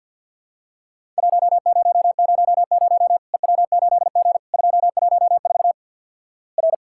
Morse code, heard as a keyed tone, about 54 wpm, 700 Hz. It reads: 1000 EW8G 214 R
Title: Worked All Europe DX Contest - QTC Exchange with Repeats - 50wpm